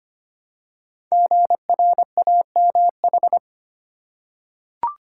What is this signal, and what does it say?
Morse code, heard as a keyed tone, about 25 wpm, 700 Hz. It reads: GRAMH
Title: Code Groups - 5 Letters - 25wpm